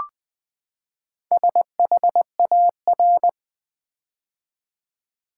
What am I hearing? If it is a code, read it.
Morse code, heard as a keyed tone, about 20 wpm, 700 Hz. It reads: SHAR
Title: Build A Word - 4 Letters At A Time - 11-Letter Words - 20wpm